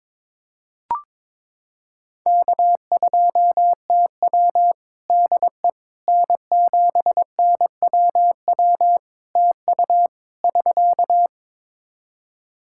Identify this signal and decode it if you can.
Morse code, heard as a keyed tone, about 22 wpm, 700 Hz. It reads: K2TW DE N7NWW TU <SK>
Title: K1USN SST QSO Practice 22wpm